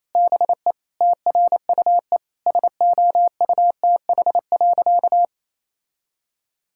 Morse code, heard as a keyed tone, about 28 wpm, 700 Hz. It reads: BE TRUE SOUTH.
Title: Sentences from Top 1000 Words - Review Part 6 - 28wpm